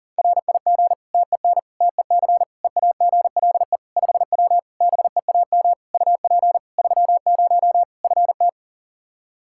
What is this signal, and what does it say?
Morse code, heard as a keyed tone, about 40 wpm, 700 Hz. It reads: RIG TEN TEC EAGLE 5W BEAM UP 30 FT
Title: Practice QSO - Exchange 2 - Info - 40wpm